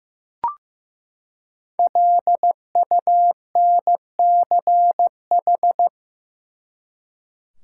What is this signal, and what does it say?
Morse code, heard as a keyed tone, about 15 wpm, 700 Hz. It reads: LUNCH
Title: Five Letter Words - 15wpm